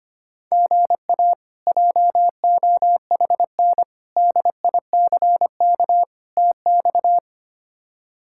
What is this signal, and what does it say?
Morse code, heard as a keyed tone, about 25 wpm, 700 Hz. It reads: GA JOHN DICK TX
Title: K1USN SST QSO Practice - 25wpm